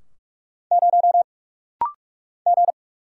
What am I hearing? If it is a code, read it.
Morse code, heard as a keyed tone, about 45 wpm, 700 Hz. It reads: 0 G
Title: Single Letter-Number Alt - Rapid-Fire 45wpm